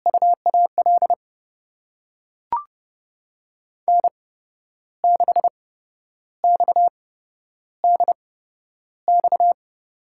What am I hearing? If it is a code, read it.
Morse code, heard as a keyed tone, about 30 wpm, spaced as for 11 wpm, 700 Hz. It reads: UAL N6XDX
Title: USA Call Signs - 5 Characters - 8x Character Spacing - 30wpm